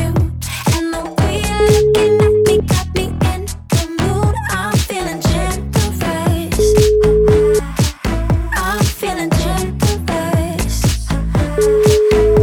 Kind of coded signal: telephone call progress tone

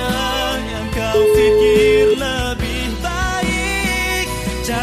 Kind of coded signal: telephone call progress tone

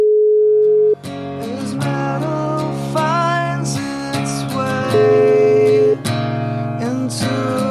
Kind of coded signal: telephone call progress tone

